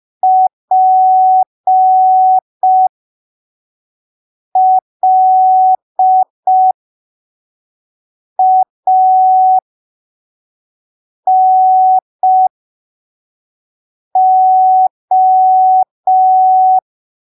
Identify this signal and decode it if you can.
Morse code, heard as a keyed tone, about 5 wpm, spaced as for 3 wpm, 750 Hz. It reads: PLANO
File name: arquivos / texto / texto 07 05ppm